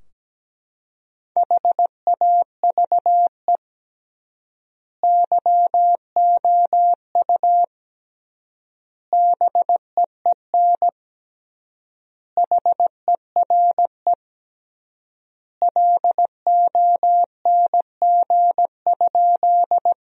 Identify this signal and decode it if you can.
Morse code, heard as a keyed tone, about 17 wpm, 700 Hz. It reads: HAVE YOU BEEN HERE LONG?
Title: 3x Word-Spacing - Sentences from Top 200 Words - 17wpm